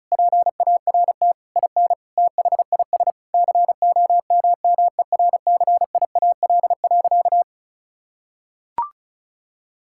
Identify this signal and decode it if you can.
Morse code, heard as a keyed tone, about 34 wpm, 700 Hz. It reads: PART IN THIS COMMERCIAL.
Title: Sentences from Top 1000 Words - Review Part 5 - 35wpm